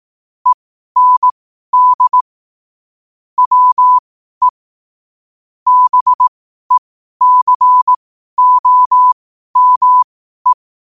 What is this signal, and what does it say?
Morse code, heard as a keyed tone, about 18 wpm, spaced as for 12 wpm, 1000 Hz. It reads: END WE BECOME
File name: qotdfiles / 12wpm